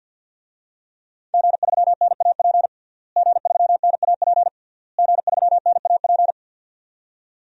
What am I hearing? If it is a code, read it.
Morse code, heard as a keyed tone, about 51 wpm, 700 Hz. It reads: G3NAP G3NAP G3NAP
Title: DXCC-100 Call Signs - 5 Characters - Repeated 3x - 50wpm